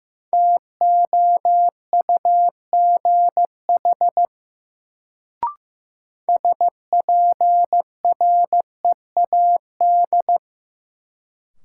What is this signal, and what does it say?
Morse code, heard as a keyed tone, about 15 wpm, 700 Hz. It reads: TOUGH SPREAD
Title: Sentences from Top 1200 Words - New Word Drill - 15wpm